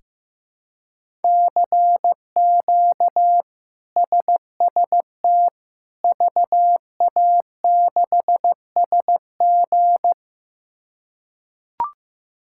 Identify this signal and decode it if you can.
Morse code, heard as a keyed tone, about 15 wpm, 700 Hz. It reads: CQ SST VA6SG
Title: K1USN SST QSO Practice - 15wpm